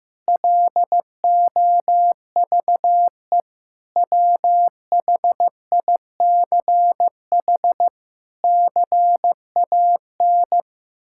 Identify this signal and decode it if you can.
Morse code, heard as a keyed tone, about 15 wpm, 700 Hz. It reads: LOVE WHICH CAN
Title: Sentences from Top 1600 Words 15wpm